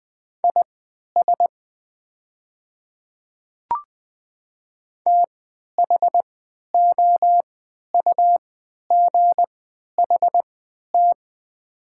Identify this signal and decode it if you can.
Morse code, heard as a keyed tone, about 20 wpm, spaced as for 11 wpm, 700 Hz. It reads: IS THOUGHT
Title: 3x Character Spacing - Top 500 Words - 20wpm